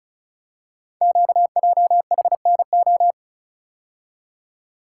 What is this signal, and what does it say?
Morse code, heard as a keyed tone, about 36 wpm, 700 Hz. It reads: QJHNO
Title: Code Groups - 5 Letters - 35wpm